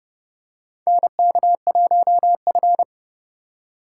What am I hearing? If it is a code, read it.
Morse code, heard as a keyed tone, about 30 wpm, 700 Hz. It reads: NK1F
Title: USA Call Signs - 4 Characters - 4x Character Spacing - 30wpm